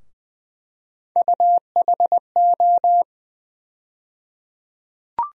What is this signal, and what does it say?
Morse code, heard as a keyed tone, about 21 wpm, 700 Hz. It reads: UHO
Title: USA Call Sign Postfixes - 3 Letters 20wpm